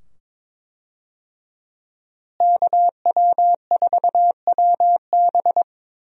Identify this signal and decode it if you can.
Morse code, heard as a keyed tone, about 22 wpm, 700 Hz. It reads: KW4WB
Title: USA Call Signs - 5 Characters - Repeated 2x - 22wpm